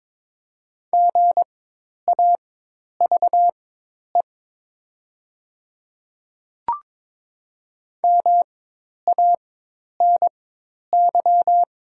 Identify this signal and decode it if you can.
Morse code, heard as a keyed tone, about 22 wpm, spaced as for 10 wpm, 690 Hz. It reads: GAVE MANY
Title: 4x Character Spacing - Top 500 Words - 22wpm